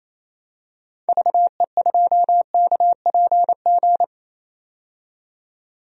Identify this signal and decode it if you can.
Morse code, heard as a keyed tone, about 28 wpm, 700 Hz. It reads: VE2KPG
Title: DXCC-100 Call Signs - 6 Characters - 28wpm